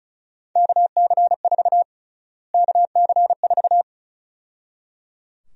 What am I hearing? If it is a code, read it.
Morse code, heard as a keyed tone, about 34 wpm, 700 Hz. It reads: KC4 KC4
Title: USA Call Sign Prefixes - Repeated 2x - 35wpm